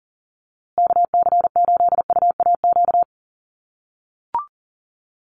Morse code, heard as a keyed tone, about 39 wpm, 700 Hz. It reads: KC8UAQ
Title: USA Call Signs Repeated 3x 40wpm